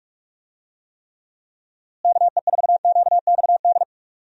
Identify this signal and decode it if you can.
Morse code, heard as a keyed tone, about 44 wpm, 700 Hz. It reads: KE4QXD
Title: USA Call Signs - 6 Characters - Repeated 3x - 45wpm